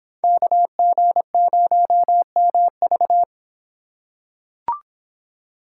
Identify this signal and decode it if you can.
Morse code, heard as a keyed tone, about 25 wpm, 700 Hz. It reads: KG0MV